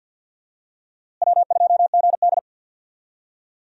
Morse code, heard as a keyed tone, about 49 wpm, 700 Hz. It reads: W2GD
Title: USA Call Signs - 4 Characters - Repeated 2x - 50wpm